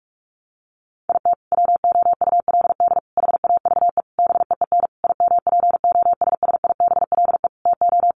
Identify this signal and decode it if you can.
Morse code, heard as a keyed tone, about 44 wpm, 700 Hz. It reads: IT WOULD HAVE BEEN IMPOSSIBLE TO
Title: Sentences from Top 1100 Words 45wpm